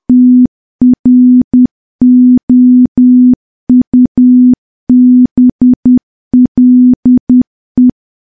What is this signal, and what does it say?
Morse code, heard as a keyed tone, about 10 wpm, 250 Hz. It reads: TROUBLE
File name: morse / morse-training-10WPM